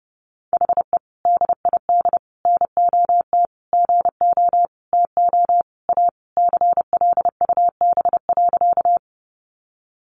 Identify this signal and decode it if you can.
Morse code, heard as a keyed tone, about 30 wpm, 700 Hz. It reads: HE DID NOT GO TO A CLUB.
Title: Sentences from Top 1200 Words 30wpm